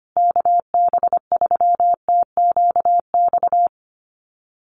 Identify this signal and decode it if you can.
Morse code, heard as a keyed tone, about 25 wpm, 700 Hz. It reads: KB3TQX
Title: USA Call Signs Repeated 2x 25wpm